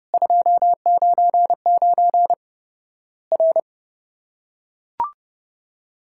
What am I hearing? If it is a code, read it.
Morse code, heard as a keyed tone, about 30 wpm, 700 Hz. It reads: 299 R
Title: Worked All Europe DX Contest - QTC Exchange - 30wpm